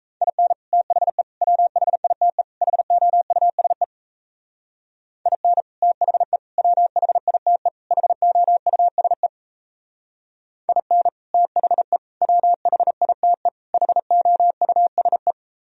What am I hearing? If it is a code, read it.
Morse code, heard as a keyed tone, about 37 wpm, 700 Hz. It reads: IN THE WHITE HOUSE IN THE WHITE HOUSE IN THE WHITE HOUSE